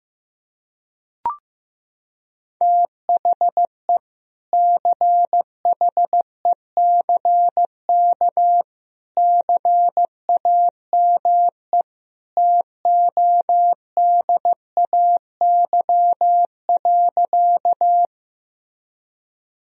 Morse code, heard as a keyed tone, about 15 wpm, 700 Hz. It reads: THE CHECK CAME TODAY.
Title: Sentences from Top 1000 Words - Review Part 3 - 15wpm